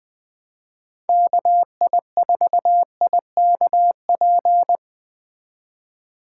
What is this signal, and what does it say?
Morse code, heard as a keyed tone, about 20 wpm, 700 Hz. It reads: KI4IKP